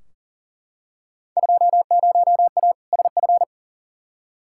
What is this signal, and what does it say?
Morse code, heard as a keyed tone, about 40 wpm, 700 Hz. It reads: 20A SF
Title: Field Day Exchanges Encore - 40wpm